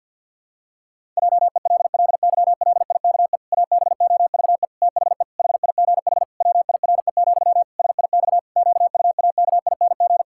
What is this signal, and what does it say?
Morse code, heard as a keyed tone, about 51 wpm, 700 Hz. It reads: JELLYLIKE ABOVE THE HIGH WIRE, SIX QUAKING
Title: Pangram Challenge v2 - 50wpm